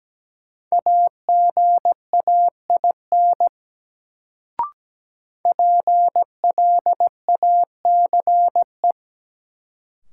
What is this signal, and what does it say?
Morse code, heard as a keyed tone, about 17 wpm, 700 Hz. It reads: AGAIN PLACE
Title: Sets of 4 Words - New Word Drill - 17wpm